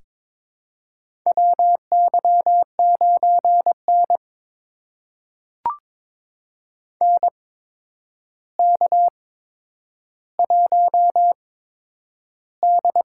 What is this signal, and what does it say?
Morse code, heard as a keyed tone, about 22 wpm, spaced as for 9 wpm, 700 Hz. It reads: WY9N NK1D
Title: USA Call Signs - 4 Characters - 8x Character Spacing - 22wpm